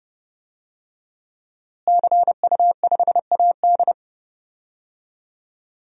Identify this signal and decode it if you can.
Morse code, heard as a keyed tone, about 30 wpm, 700 Hz. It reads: CU5AD
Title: DXCC-100 Call Signs - 5 Characters - Repeated 2x - 30wpm